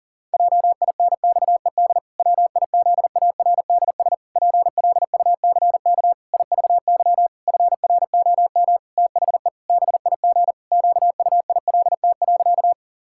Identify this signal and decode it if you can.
Morse code, heard as a keyed tone, about 40 wpm, 700 Hz. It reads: JINXED WIZARDS PLUCK IVY FROM THE BIG QUILT.
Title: Pangram Challenge v2 - 40wpm